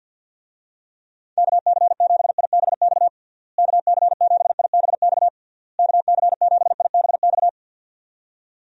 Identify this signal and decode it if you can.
Morse code, heard as a keyed tone, about 52 wpm, 700 Hz. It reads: KC7IBX KC7IBX KC7IBX
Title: USA Call Signs - 6 Characters - Repeated 3x - 50wpm